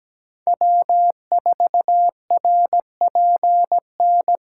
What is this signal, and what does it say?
Morse code, heard as a keyed tone, about 17 wpm, 700 Hz. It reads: W4RPN
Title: USA Call Signs - 2x Character Spacing - 17wpm